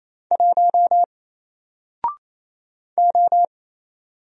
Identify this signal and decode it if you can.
Morse code, heard as a keyed tone, about 27 wpm, 700 Hz. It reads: 1 O
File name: Single-Letter-Number-28wpm